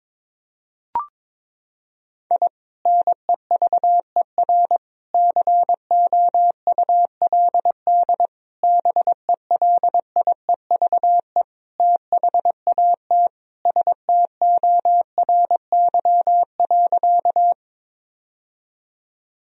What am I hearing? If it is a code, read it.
Morse code, heard as a keyed tone, about 22 wpm, 700 Hz. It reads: I NEVER COULD BELIEVE THAT STORY.
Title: Sentences from Top 1000 Words - Review Part 1 - 22wpm